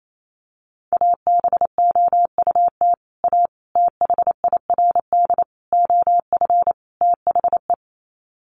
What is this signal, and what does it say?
Morse code, heard as a keyed tone, about 28 wpm, 700 Hz. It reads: ABOUT A THIRD OF THE
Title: Sets of 5 Words Encore2 28wpm